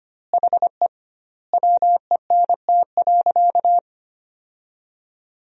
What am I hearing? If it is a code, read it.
Morse code, heard as a keyed tone, about 25 wpm, 700 Hz. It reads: HE WENT.
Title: 2x Word-Spacing - Sentences from Top 200 Words - 25wpm